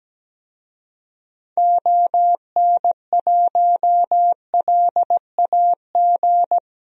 Morse code, heard as a keyed tone, about 17 wpm, 700 Hz. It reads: ON1LAG
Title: DXCC-100 Call Signs - 6 Characters - Repeated 3x - 17wpm